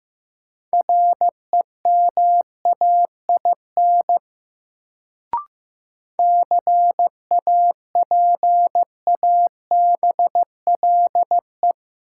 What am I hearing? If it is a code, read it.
Morse code, heard as a keyed tone, about 15 wpm, 700 Hz. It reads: REMAIN CAPABLE
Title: Sentences from Top 1200 Words - New Word Drill - 15wpm